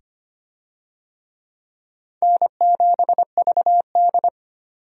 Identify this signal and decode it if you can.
Morse code, heard as a keyed tone, about 25 wpm, 700 Hz. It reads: N7VD
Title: USA Call Signs - 4 Characters - 25wpm